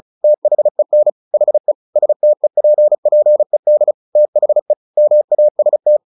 Morse code, heard as a keyed tone, about 34 wpm, 600 Hz. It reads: THEN HE STEPPED THE MAST
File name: the old man and the sea 2 35wpm